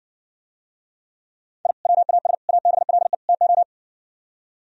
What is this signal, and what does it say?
Morse code, heard as a keyed tone, about 70 wpm, 700 Hz. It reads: I WAS ABLE TO